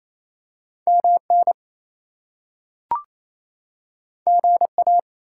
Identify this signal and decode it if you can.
Morse code, heard as a keyed tone, about 28 wpm, 700 Hz. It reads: MN GA
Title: US States and Canadian Provinces 28wpm